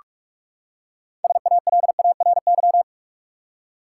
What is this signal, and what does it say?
Morse code, heard as a keyed tone, about 46 wpm, 700 Hz. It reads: SALARY
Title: Six Letter Words - 45wpm